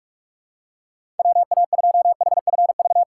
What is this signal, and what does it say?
Morse code, heard as a keyed tone, about 46 wpm, 700 Hz. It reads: WA2HFV
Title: USA Call Signs - 6 Characters - 2x Character Spacing - 45wpm